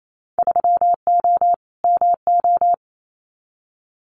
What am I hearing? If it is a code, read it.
Morse code, heard as a keyed tone, about 28 wpm, 700 Hz. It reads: 3O MO
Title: Winter Field Day Exchanges Encore 28wpm